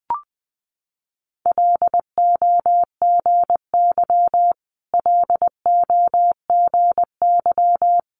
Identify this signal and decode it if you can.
Morse code, heard as a keyed tone, about 20 wpm, 700 Hz. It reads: LOGY LOGY
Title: Multi-letter Phonemes Repeated 3x 20wpm